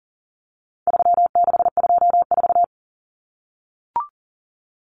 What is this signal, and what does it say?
Morse code, heard as a keyed tone, about 40 wpm, 700 Hz. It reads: 3624
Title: Common Cut Numbers - 4-Digit 40wpm